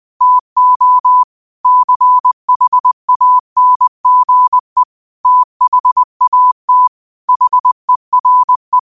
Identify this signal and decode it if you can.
Morse code, heard as a keyed tone, about 20 wpm, 1000 Hz. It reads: TO CHANGE THAT HERE
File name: qotdfiles / 20wpm